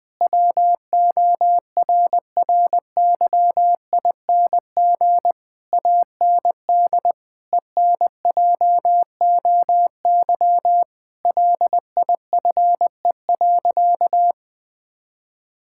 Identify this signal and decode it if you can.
Morse code, heard as a keyed tone, about 20 wpm, 700 Hz. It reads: WORRYING AND ENJOY LIFE.
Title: Sentences with 8-Word Phrases - 20wpm